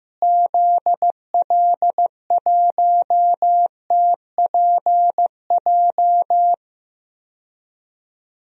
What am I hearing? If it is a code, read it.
Morse code, heard as a keyed tone, about 15 wpm, 700 Hz. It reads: ZL1TPJ
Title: DXCC-100 Call Signs - 6 Characters - Repeated 3x - 15wpm